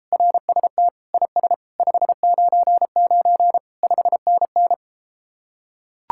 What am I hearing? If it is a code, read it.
Morse code, heard as a keyed tone, about 33 wpm, 700 Hz. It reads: RST IS 599 5NN